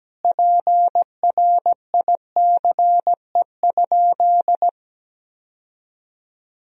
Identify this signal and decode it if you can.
Morse code, heard as a keyed tone, about 17 wpm, 700 Hz. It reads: PRICE?
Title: Sentences from Top 1000 Words - Review Part 3 - 17wpm